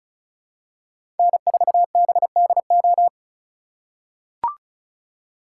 Morse code, heard as a keyed tone, about 36 wpm, 700 Hz. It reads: N4BDO
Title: USA Call Signs - 4x Character Spacing - 35wpm